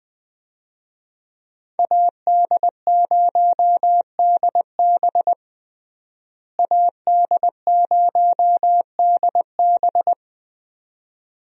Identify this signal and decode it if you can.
Morse code, heard as a keyed tone, about 20 wpm, 700 Hz. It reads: AD0DB AD0DB
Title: USA Call Signs - 5 Characters - Repeated 2x - 20wpm